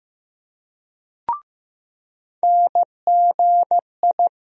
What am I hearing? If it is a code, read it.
Morse code, heard as a keyed tone, about 15 wpm, 700 Hz. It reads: NGI
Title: USA Call Sign Postfixes - 3 Letters 15wpm